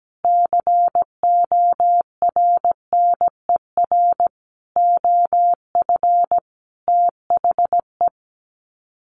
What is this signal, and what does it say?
Morse code, heard as a keyed tone, about 17 wpm, 700 Hz. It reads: CORNER OF THE
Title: Sets of 5 Words Encore 17wpm